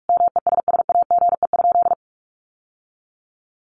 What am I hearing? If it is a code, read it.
Morse code, heard as a keyed tone, about 44 wpm, 700 Hz. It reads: MESSAGE?
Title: Sentences from Top 1100 Words 45wpm